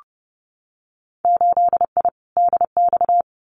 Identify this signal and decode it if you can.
Morse code, heard as a keyed tone, about 30 wpm, 700 Hz. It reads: 8I DX
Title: Winter Field Day Exchanges Encore 30wpm